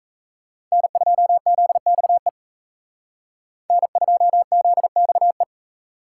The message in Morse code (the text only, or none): N2ZXE N2ZXE